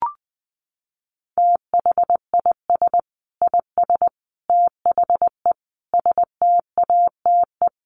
Morse code, heard as a keyed tone, about 20 wpm, 700 Hz. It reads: THIS IS THE STATE
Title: Sentences from Top 1400 Words 20wpm